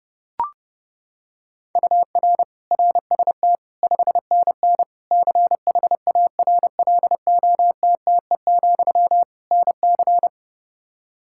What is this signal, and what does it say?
Morse code, heard as a keyed tone, about 30 wpm, 700 Hz. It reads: UR RST 5NN CHARLOTTE, NC
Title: K1USN SST QSO Practice 30wpm